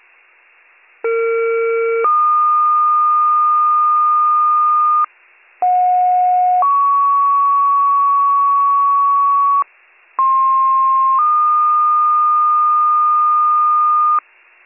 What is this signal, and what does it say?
Two-tone page, heard as a one-second tone, then a three-second tone, about 460 then 1220 Hz, about 730 then 1120 Hz, about 1060 then 1220 Hz. It is heard from 1.0 to 14.2 s.